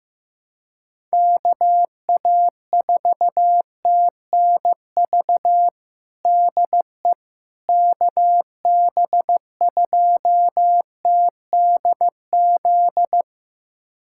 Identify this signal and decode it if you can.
Morse code, heard as a keyed tone, about 15 wpm, 700 Hz. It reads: KA4TNV DE KB2TDZ
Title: USA Callsign DE Callsign - 6 Characters - 15wpm